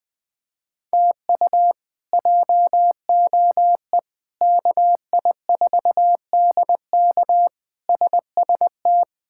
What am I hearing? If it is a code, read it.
Morse code, heard as a keyed tone, about 20 wpm, 700 Hz. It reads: TU JOE KI4DK SST
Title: K1USN SST QSO Practice - 20wpm